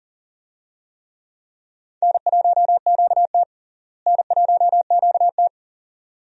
Morse code, heard as a keyed tone, about 40 wpm, 700 Hz. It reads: N1QT N1QT
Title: USA Call Signs - 4 Characters - Repeated 2x - 40wpm